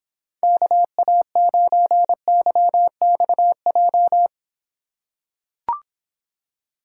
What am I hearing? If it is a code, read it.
Morse code, heard as a keyed tone, about 26 wpm, 700 Hz. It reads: KA9YXJ